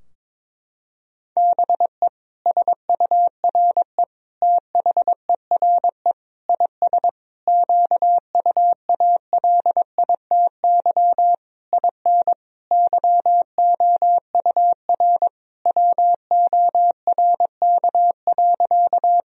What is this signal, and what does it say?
Morse code, heard as a keyed tone, about 21 wpm, 700 Hz. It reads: BE SURE THERE IS QUALITY IN YOUR WORK.
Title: Sentences from Top 1000 Words - Review Part 4 - 22wpm